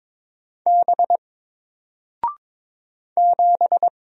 Morse code, heard as a keyed tone, about 22 wpm, 700 Hz. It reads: B 7
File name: Single-Letter-Number-22wpm